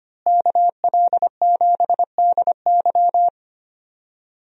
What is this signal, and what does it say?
Morse code, heard as a keyed tone, about 25 wpm, 700 Hz. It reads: KL7DY
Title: DXCC-100 Call Signs - 5 Characters - Repeated 2x - 25wpm